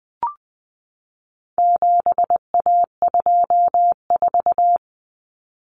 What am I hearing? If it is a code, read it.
Morse code, heard as a keyed tone, about 20 wpm, 700 Hz. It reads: 7A24
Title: Common Cut Numbers - 4-Digit 20wpm